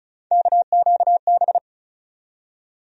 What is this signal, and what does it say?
Morse code, heard as a keyed tone, about 36 wpm, 700 Hz. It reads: KQB
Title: USA Call Sign Postfixes - 3 Letters 35wpm